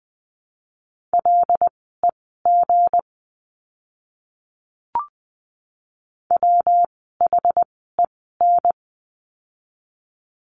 Morse code, heard as a keyed tone, about 20 wpm, spaced as for 15 wpm, 700 Hz. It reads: LEG WHEN
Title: 160 Lesson 26f - Letter g - Words 20wpm